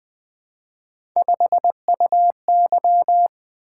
Morse code, heard as a keyed tone, about 20 wpm, 700 Hz. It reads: 5UY
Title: Three Letter-Number Combinations - 20wpm